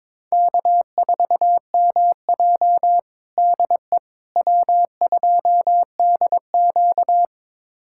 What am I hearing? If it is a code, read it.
Morse code, heard as a keyed tone, about 22 wpm, 700 Hz. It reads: K4MJ DE W2DQ
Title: USA Callsign DE Callsign - 4 Characters - 22wpm